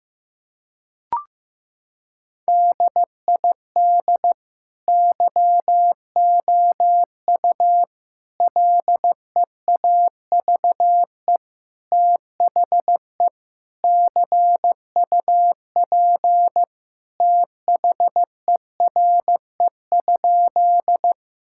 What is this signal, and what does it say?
Morse code, heard as a keyed tone, about 15 wpm, 700 Hz. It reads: DID YOU LEAVE THE CUP THERE?
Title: Sentences from Top 1600 Words 15wpm